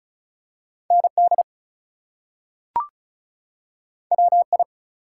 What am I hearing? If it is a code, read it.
Morse code, heard as a keyed tone, about 36 wpm, 700 Hz. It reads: ND WI
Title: US States and Canadian Provinces 35wpm